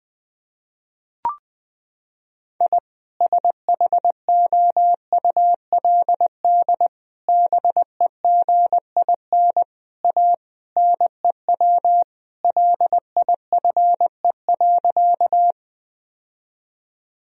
Morse code, heard as a keyed tone, about 20 wpm, 700 Hz. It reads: I SHOULD BEGIN A NEW LIFE.
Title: Sentences from Top 1000 Words - Review Part 1 - 20wpm